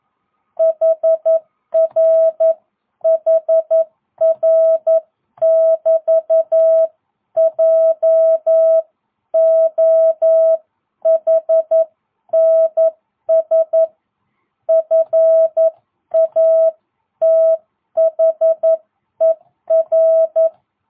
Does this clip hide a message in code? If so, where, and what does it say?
Yes: Morse code from 0.6 to 20.5 s, heard as a keyed tone, about 11 wpm, 660 Hz. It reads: HR HR <BT> JOHNS FATHER